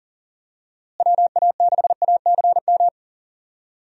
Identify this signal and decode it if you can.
Morse code, heard as a keyed tone, about 40 wpm, 700 Hz. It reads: WA6ACM
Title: USA Call Signs - 6 Characters - Repeated 3x - 40wpm